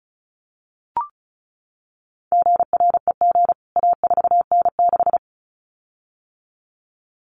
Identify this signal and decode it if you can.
Morse code, heard as a keyed tone, about 34 wpm, 700 Hz. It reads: GREG A4N6
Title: CWT Exchange 35wpm